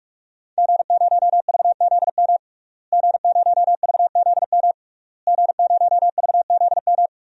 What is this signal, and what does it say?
Morse code, heard as a keyed tone, about 41 wpm, 700 Hz. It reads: G0VZM G0VZM G0VZM
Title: DXCC-100 Call Signs - 5 Characters - Repeated 3x - 45wpm